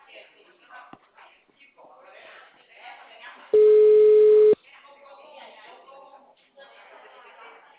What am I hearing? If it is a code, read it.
Ringback tone (Europe, 425 Hz), with speech.